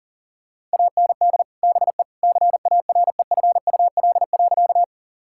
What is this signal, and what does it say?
Morse code, heard as a keyed tone, about 37 wpm, 700 Hz. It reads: AND BE CAREFUL.
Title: Sentences from Top 1600 Words 40wpm